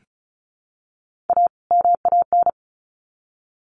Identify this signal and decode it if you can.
Morse code, heard as a keyed tone, about 34 wpm, 700 Hz. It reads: A MAN
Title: Sets of 2 Words 35wpm